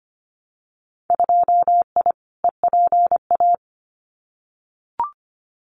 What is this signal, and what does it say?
Morse code, heard as a keyed tone, about 25 wpm, 700 Hz. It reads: 2I EPA
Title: Winter Field Day Exchanges Encore 25wpm